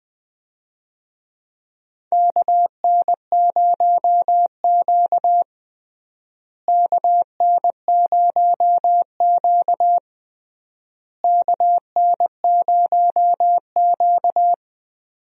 Morse code, heard as a keyed tone, about 20 wpm, 700 Hz. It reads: KN0Q KN0Q KN0Q
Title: USA Call Signs - 4 Characters - Repeated 3x - 20wpm